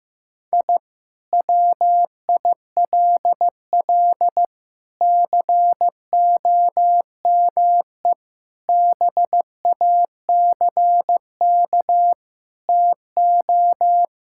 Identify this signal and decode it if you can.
Morse code, heard as a keyed tone, about 15 wpm, 700 Hz. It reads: I WILL COME BACK TO
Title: Sentences from Top 1000 Words - Review Part 6 - 15wpm